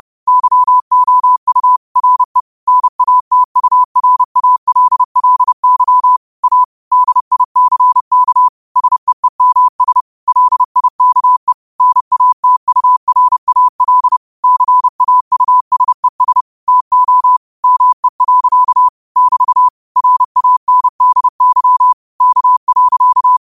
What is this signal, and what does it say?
Morse code, heard as a keyed tone, about 30 wpm, 1000 Hz. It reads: YOU RE NATURALLY A DICK SEEMS LIKE NATURAL CAUSES TO ME. <BT> RANDY K.